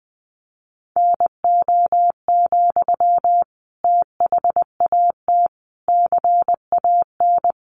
Morse code, heard as a keyed tone, about 20 wpm, 700 Hz. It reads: NO, THAT CAN
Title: Sentences from Top 300 Words 20wpm